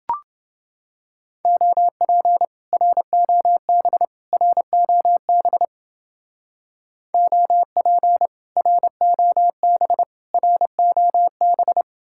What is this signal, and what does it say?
Morse code, heard as a keyed tone, about 30 wpm, 700 Hz. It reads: OP ROB ROB OP ROB ROB